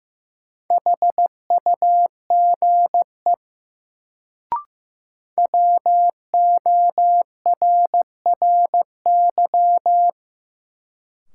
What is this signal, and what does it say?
Morse code, heard as a keyed tone, about 15 wpm, 700 Hz. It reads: HUGE WORRY
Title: Sentences from Top 1100 Words - New Word Drill - 15wpm